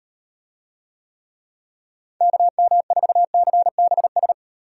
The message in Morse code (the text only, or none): KM4CBS